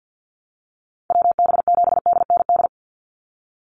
Morse code, heard as a keyed tone, about 50 wpm, 700 Hz. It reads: WB7DND
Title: Call Signs Repeated 2x - 50wpm